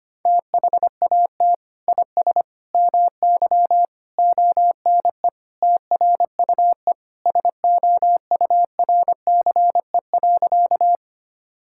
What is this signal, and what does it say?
Morse code, heard as a keyed tone, about 25 wpm, 700 Hz. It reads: THAT IS MY ONE TRUE SOURCE.
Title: Sentences from Top 1000 Words - Review Part 5 - 25wpm